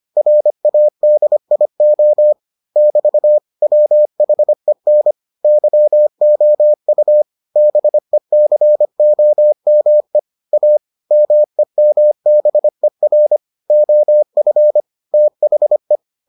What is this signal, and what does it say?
Morse code, heard as a keyed tone, about 25 wpm, 600 Hz. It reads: RADIO <BT> WHEN YOU BECOME A MEMBER OF THE